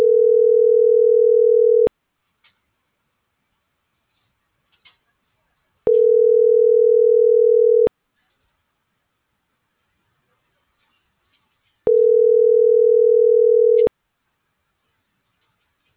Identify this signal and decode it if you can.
Ringback tone (North America), 440 + 480 Hz.